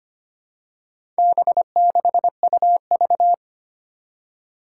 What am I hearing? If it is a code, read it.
Morse code, heard as a keyed tone, about 25 wpm, 700 Hz. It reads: B6UV
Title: Four Challenging Letter-Number Combinations 25wpm